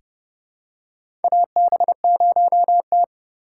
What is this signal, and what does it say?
Morse code, heard as a keyed tone, about 30 wpm, 700 Hz. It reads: AB0T
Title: USA Call Signs - 4 Characters - 3x Character Spacing - 30wpm